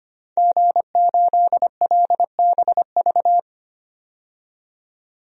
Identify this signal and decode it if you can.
Morse code, heard as a keyed tone, about 25 wpm, 700 Hz. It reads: G8LBV
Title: DXCC-100 Call Signs - 5 Characters - 25wpm